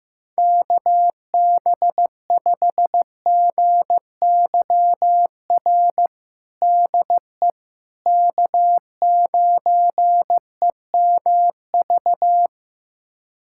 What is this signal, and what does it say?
Morse code, heard as a keyed tone, about 16 wpm, 700 Hz. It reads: KB5GYR DE K9EMV
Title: USA Callsign DE Callsign 15wpm